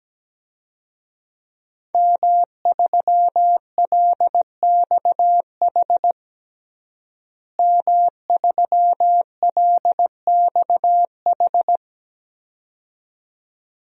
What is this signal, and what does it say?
Morse code, heard as a keyed tone, about 17 wpm, 700 Hz. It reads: M3LXH M3LXH
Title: DXCC-100 Call Signs - 5 Characters - Repeated 2x - 17wpm